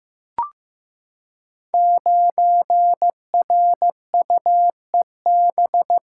Morse code, heard as a keyed tone, about 15 wpm, 700 Hz. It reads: 9RUEB
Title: Code Groups - 5 Letter:Numbers - 15wpm